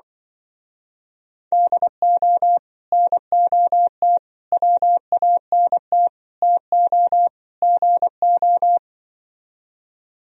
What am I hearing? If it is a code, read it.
Morse code, heard as a keyed tone, about 24 wpm, 700 Hz. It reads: DO NOT WANT TO GO
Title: Speed-Racing - Sets of 5 Words 20wpm